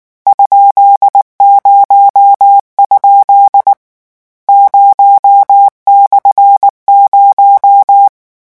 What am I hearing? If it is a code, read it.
Morse code, heard as a keyed tone, about 19 wpm, 800 Hz. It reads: ?0? 0/0
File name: morse mp3 / lecon13-vitesse 18